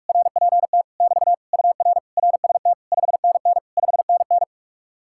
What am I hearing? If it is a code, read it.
Morse code, heard as a keyed tone, about 69 wpm, spaced as for 56 wpm, 700 Hz. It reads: RPT <BT> UR RST 5NN 5NN
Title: Speed-Racing Common QSO Phrases Encore - 45wpm